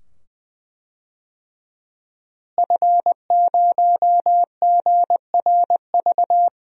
Morse code, heard as a keyed tone, about 20 wpm, 700 Hz. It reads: F0GRV
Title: DXCC-100 Call Signs - 5 Characters - Repeated 2x - 20wpm